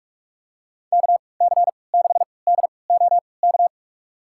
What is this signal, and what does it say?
Morse code, heard as a keyed tone, about 41 wpm, spaced as for 25 wpm, 700 Hz. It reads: KC6DOK
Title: USA Call Signs - 6 Characters - 3x Character Spacing - 45wpm